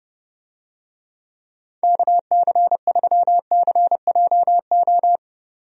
Morse code, heard as a keyed tone, about 32 wpm, 700 Hz. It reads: KC3CJO
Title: USA Call Signs - 6 Characters - 30wpm